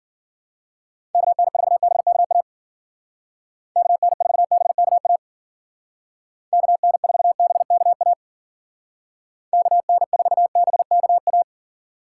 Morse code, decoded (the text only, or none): KN4BKA KN4BKA KN4BKA KN4BKA